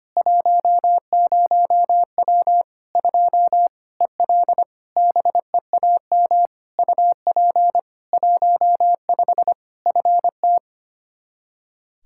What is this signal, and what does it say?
Morse code, heard as a keyed tone, about 25 wpm, 700 Hz. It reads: 10W 2 EL BEAM UP 15 FT
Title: Practice QSO - Exchange 2 - Info - 25wpm